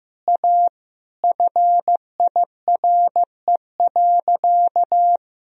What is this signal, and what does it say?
Morse code, heard as a keyed tone, about 15 wpm, 700 Hz. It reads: A FIRE.
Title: Sentences from Top 1000 Words - Review Part 4 - 15wpm